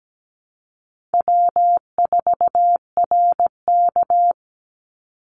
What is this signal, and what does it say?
Morse code, heard as a keyed tone, about 17 wpm, 700 Hz. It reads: W4RK
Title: Call Signs Repeated 2x - 17wpm